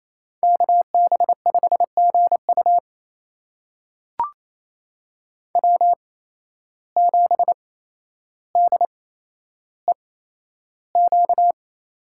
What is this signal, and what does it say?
Morse code, heard as a keyed tone, about 28 wpm, spaced as for 12 wpm, 700 Hz. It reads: KB5GU W7DEQ
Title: USA Call Signs - 5 Characters - 8x Character Spacing - 28wpm